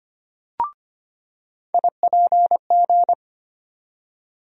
Morse code, heard as a keyed tone, about 25 wpm, 700 Hz. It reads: IPG
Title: USA Call Sign Postfixes - 3 Letters 25wpm